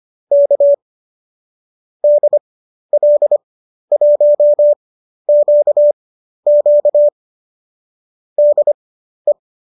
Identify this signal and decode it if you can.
Morse code, heard as a keyed tone, about 25 wpm, spaced as for 12 wpm, 590 Hz. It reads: K DL1QQ DE
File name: qso012 12